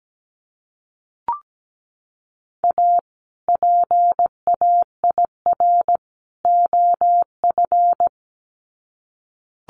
Morse code, heard as a keyed tone, about 17 wpm, 700 Hz. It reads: A PAIR OF
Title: Sets of 3 Words 17wpm